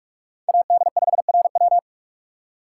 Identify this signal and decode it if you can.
Morse code, heard as a keyed tone, about 46 wpm, 700 Hz. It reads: AD5RW
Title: USA Call Signs - 5 Characters - 4x Character Spacing - 45wpm